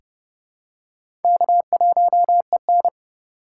Morse code, heard as a keyed tone, about 30 wpm, 700 Hz. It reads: K1EN
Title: USA Call Signs - 6x Character Spacing - 30wpm